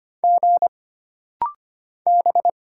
Morse code, heard as a keyed tone, about 25 wpm, 700 Hz. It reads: G B
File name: Single Character (Letter:Number:Punctuation:Prosign) - Rapid-Fire - 25wpm